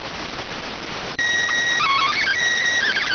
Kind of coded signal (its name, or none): SSTV (slow-scan television)